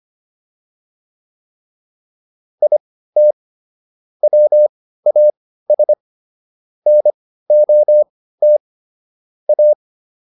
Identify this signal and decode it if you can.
Morse code, heard as a keyed tone, about 25 wpm, spaced as for 15 wpm, 600 Hz. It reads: IT WAS NOT A
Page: intermediate-practice-files